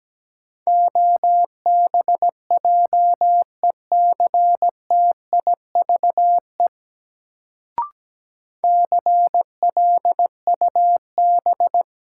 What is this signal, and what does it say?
Morse code, heard as a keyed tone, about 17 wpm, 700 Hz. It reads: OBJECTIVE CLUB
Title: Sentences from Top 1200 Words - New Word Drill - 17wpm